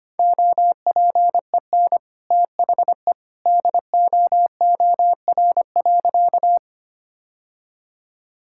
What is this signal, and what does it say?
Morse code, heard as a keyed tone, about 25 wpm, 700 Hz. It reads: OPEN THE DOOR.
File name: practice / 25wpm / Sentences with 2-Word Phrases - 25wpm